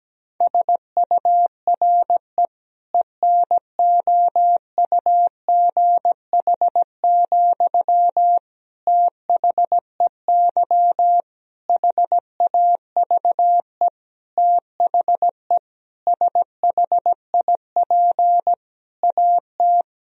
Morse code, heard as a keyed tone, about 17 wpm, 700 Hz. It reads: SURE ENOUGH, THEY HAVE THE SHIP AT
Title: Sentences from Top 1000 Words - Review Part 2 - 17wpm